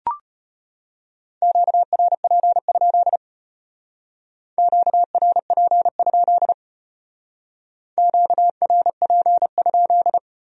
Morse code, heard as a keyed tone, about 31 wpm, 700 Hz. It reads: QRP? QRP? QRP?
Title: Speed-Racing Q-Codes - 25wpm